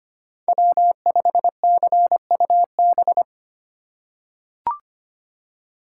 Morse code, heard as a keyed tone, about 25 wpm, 700 Hz. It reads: W5CUB
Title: USA Call Signs - 5 Characters - 4x Character Spacing - 25wpm